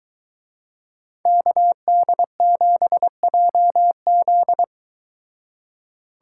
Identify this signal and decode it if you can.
Morse code, heard as a keyed tone, about 23 wpm, 700 Hz. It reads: KD7JZ